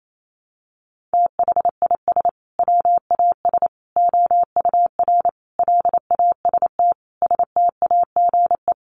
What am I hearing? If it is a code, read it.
Morse code, heard as a keyed tone, about 28 wpm, 700 Hz. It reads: THIS WAS OUR LAST STAGE
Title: Sentences from Top 1200 Words 28wpm